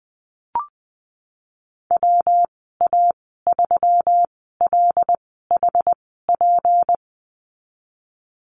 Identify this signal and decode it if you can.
Morse code, heard as a keyed tone, about 20 wpm, spaced as for 15 wpm, 700 Hz. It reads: WA3LHP
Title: 231 Lesson 37g - Letter x - Call Signs 20wpm